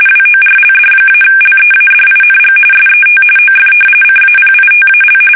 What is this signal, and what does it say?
RTTY, about 45.45 baud. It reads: WORLD-LEADING SET OF CAPITAL
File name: fsk850